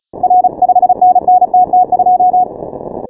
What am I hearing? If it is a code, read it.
Morse code, heard as a keyed tone, about 35 wpm, 730 Hz. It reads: R 5NN TT2
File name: final / W1UJ-168-140-MorseRunner